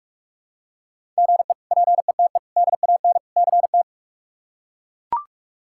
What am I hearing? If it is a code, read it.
Morse code, heard as a keyed tone, about 41 wpm, 700 Hz. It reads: GE PETE DAN CT
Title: K1USN SST QSO Practice - 45wpm